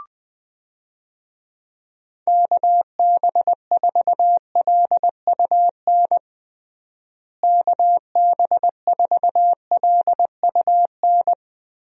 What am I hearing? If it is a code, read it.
Morse code, heard as a keyed tone, about 20 wpm, 700 Hz. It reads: KB4LUN KB4LUN
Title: USA Call Signs - 6 Characters - Repeated 2x - 20wpm